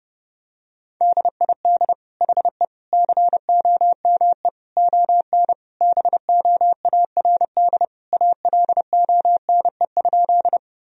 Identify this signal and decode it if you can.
Morse code, heard as a keyed tone, about 28 wpm, 700 Hz. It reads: DID HE COME ON BOARD ALONE?
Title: Sentences from Top 1000 Words - Review Part 4 - 30wpm